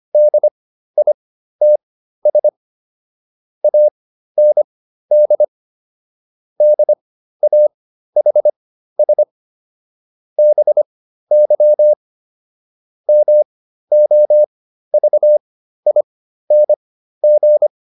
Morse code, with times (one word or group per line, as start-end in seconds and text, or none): 0.1-2.5 DITS
3.6-5.4 AND
6.6-9.2 DAHS
10.4-11.9 BY
13.1-17.7 MOVING